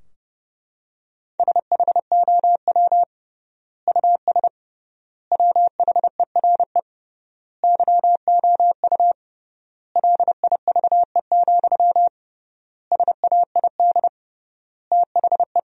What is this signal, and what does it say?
Morse code, heard as a keyed tone, about 30 wpm, 700 Hz. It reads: SHOW US WHERE YOU LIVE, SAID THE
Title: 3x Word-Spacing - Sentences from Top 200 Words - 30wpm